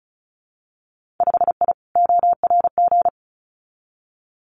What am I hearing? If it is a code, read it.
Morse code, heard as a keyed tone, about 36 wpm, 700 Hz. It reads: 5I ORG
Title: Winter Field Day Exchanges Encore 35wpm